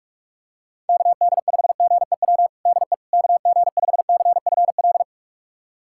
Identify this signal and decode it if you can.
Morse code, heard as a keyed tone, about 42 wpm, 700 Hz. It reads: KD5GEW DE KG5CFL
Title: USA Callsign DE Callsign - 6 Characters - 45wpm